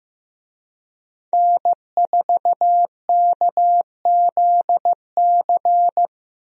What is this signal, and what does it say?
Morse code, heard as a keyed tone, about 15 wpm, 700 Hz. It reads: N4KZC
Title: USA Call Signs - 5 Characters - Repeated 3x - 15wpm